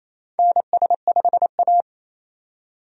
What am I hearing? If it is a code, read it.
Morse code, heard as a keyed tone, about 28 wpm, 700 Hz. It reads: NS5A
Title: USA Call Signs - 4 Characters - 28wpm